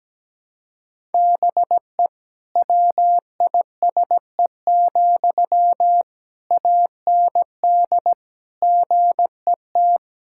Morse code, heard as a keyed tone, about 17 wpm, 700 Hz. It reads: BE WISE, AND GET
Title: Sentences from Top 1600 Words 17wpm